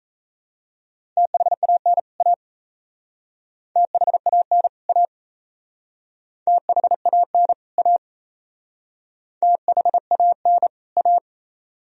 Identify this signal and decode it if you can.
Morse code, heard as a keyed tone, about 35 wpm, 700 Hz. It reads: THAN A THAN A THAN A THAN A
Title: Speed-Racing - Sets of 2 Words 28wpm